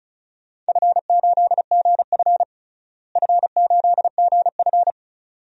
Morse code, heard as a keyed tone, about 36 wpm, 700 Hz. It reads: F8GF F8GF
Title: DXCC-100 Call Signs - 4 Characters - Repeated 2x - 35wpm